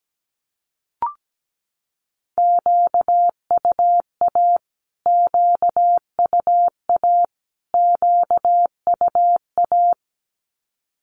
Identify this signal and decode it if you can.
Morse code, heard as a keyed tone, about 17 wpm, 700 Hz. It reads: QUA QUA QUA
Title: Q-Codes Repeated 3x 17wpm